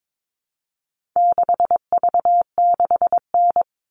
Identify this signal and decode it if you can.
Morse code, heard as a keyed tone, about 22 wpm, 700 Hz. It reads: 6V6N
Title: All Cut Numbers - 4-Digit 22wpm